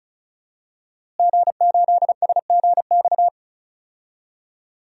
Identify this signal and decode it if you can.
Morse code, heard as a keyed tone, about 36 wpm, 700 Hz. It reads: G8SGX
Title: DXCC-100 Call Signs - 5 Characters - Repeated 3x - 35wpm